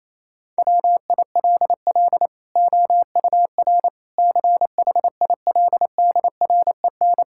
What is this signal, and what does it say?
Morse code, heard as a keyed tone, about 28 wpm, 700 Hz. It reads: WILL OUR CHILDREN
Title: Sentences from Top 1000 Words - Review Part 5 - 28wpm